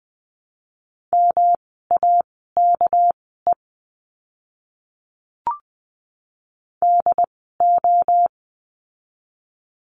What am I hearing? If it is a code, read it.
Morse code, heard as a keyed tone, about 20 wpm, spaced as for 14 wpm, 700 Hz. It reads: MAKE DO
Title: 230 Lesson 37f - Letter x - Words 20wpm